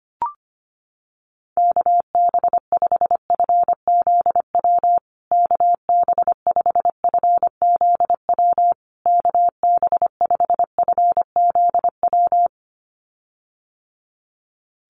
Morse code, heard as a keyed tone, about 25 wpm, 700 Hz. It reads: KB5FZW KB5FZW KB5FZW
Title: USA Call Signs Repeated 3x 25wpm